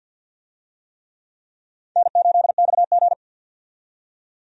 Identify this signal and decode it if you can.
Morse code, heard as a keyed tone, about 47 wpm, 700 Hz. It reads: N8XG
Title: USA Call Signs - 4 Characters - 50wpm